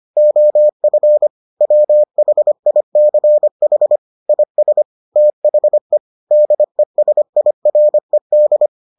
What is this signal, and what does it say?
Morse code, heard as a keyed tone, about 25 wpm, 600 Hz. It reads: OF WHICH IS THE DESIRED